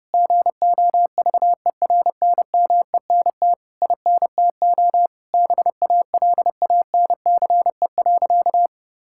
Morse code, heard as a keyed tone, about 32 wpm, 700 Hz. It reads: GOVERNMENT INTO BALANCE.
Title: Sentences from Top 1000 Words - Review Part 6 - 30wpm